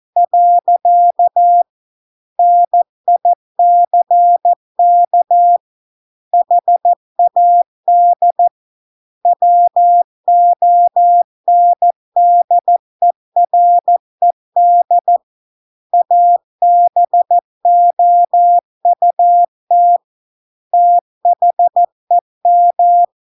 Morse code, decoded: . NICK HAD WONDERED ABOUT THEM

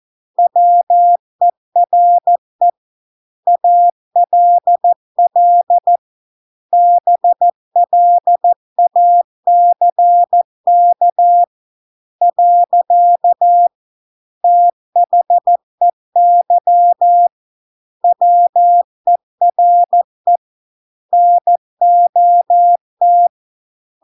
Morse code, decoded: WERE ALL BLACK . THEY WERE NOT

